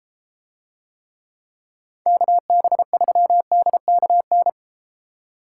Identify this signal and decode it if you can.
Morse code, heard as a keyed tone, about 33 wpm, 700 Hz. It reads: KB3DKN